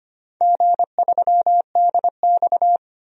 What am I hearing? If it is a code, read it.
Morse code, heard as a keyed tone, about 25 wpm, 700 Hz. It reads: G3DX
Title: DXCC-100 Call Signs - 4 Characters - 25wpm